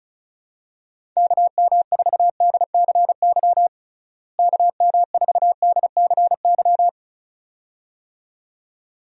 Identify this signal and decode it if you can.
Morse code, heard as a keyed tone, about 36 wpm, 700 Hz. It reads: KM4DCY KM4DCY
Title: USA Call Signs - 6 Characters - Repeated 2x - 35wpm